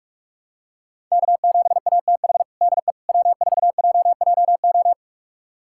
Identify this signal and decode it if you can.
Morse code, heard as a keyed tone, about 46 wpm, 700 Hz. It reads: K7ATH DE W4JJO
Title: USA Callsign DE Callsign - 5 Characters - 45wpm